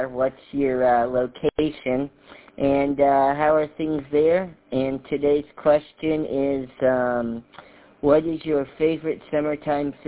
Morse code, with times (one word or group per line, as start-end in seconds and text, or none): none